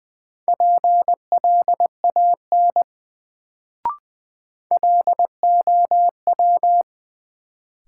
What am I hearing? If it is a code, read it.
Morse code, heard as a keyed tone, about 20 wpm, 700 Hz. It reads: PLAN LOW
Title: Sentences from Top 500 Words - New Word Drill - 20wpm